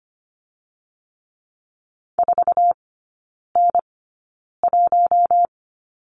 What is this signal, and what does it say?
Morse code, heard as a keyed tone, about 25 wpm, spaced as for 9 wpm, 700 Hz. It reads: 4N1
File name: Lesson 17e - Letter u - Three Characters 25wpm